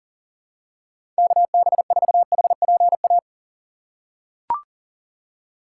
Morse code, heard as a keyed tone, about 40 wpm, 700 Hz. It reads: KB4HPA
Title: USA Call Signs - 6 Characters - 6x Character Spacing - 40wpm